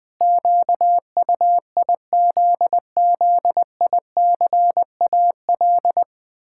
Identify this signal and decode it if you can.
Morse code, heard as a keyed tone, about 20 wpm, 700 Hz. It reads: QUIZZICAL
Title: Pangram Challenge v2 - 20wpm